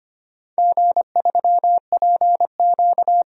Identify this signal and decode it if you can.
Morse code, heard as a keyed tone, about 25 wpm, 700 Hz. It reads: G3PQ
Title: DXCC-100 Call Signs - 4 Characters - 25wpm